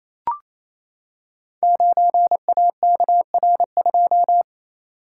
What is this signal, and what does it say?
Morse code, heard as a keyed tone, about 28 wpm, 700 Hz. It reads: 9AKR2
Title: Code Groups - 5 Letter:Numbers - 28wpm